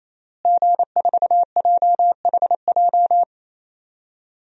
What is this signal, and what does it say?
Morse code, heard as a keyed tone, about 27 wpm, 700 Hz. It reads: G4JHJ
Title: DXCC-100 Call Signs - 5 Characters - Repeated 3x - 28wpm